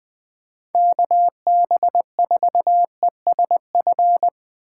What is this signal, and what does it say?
Morse code, heard as a keyed tone, about 22 wpm, 700 Hz. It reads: KB4ESF
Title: Speed Racing - USA 2x3 Call Signs 20wpm